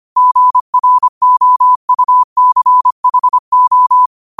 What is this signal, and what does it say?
Morse code, heard as a keyed tone, about 25 wpm, 1000 Hz. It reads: GROUCHO
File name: qotdfiles / 25wpm